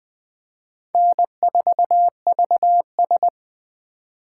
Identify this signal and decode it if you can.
Morse code, heard as a keyed tone, about 21 wpm, 700 Hz. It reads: N4VS
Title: USA Call Signs - 4 Characters - 2x Character Spacing - 20wpm